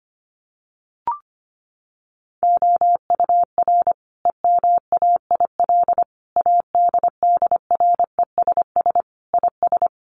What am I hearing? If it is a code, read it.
Morse code, heard as a keyed tone, about 25 wpm, 700 Hz. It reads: OUR EMAIL ADDRESS IS
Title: Sets of 5 Words Encore2 25wpm